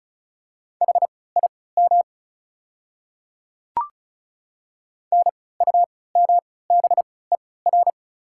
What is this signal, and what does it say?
Morse code, heard as a keyed tone, about 36 wpm, spaced as for 20 wpm, 690 Hz. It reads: HIM NUMBER
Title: 3x Character Spacing - Top 100 Words - 35wpm